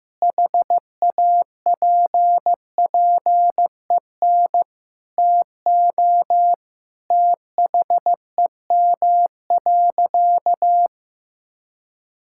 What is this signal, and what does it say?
Morse code, heard as a keyed tone, about 15 wpm, 700 Hz. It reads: HAPPEN TO THEM.
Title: Sentences from Top 1000 Words - Review Part 5 - 15wpm